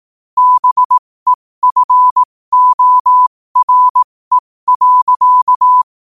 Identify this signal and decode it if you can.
Morse code, heard as a keyed tone, about 18 wpm, spaced as for 15 wpm, 1000 Hz. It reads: BEFORE.
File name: qotdfiles / 15wpm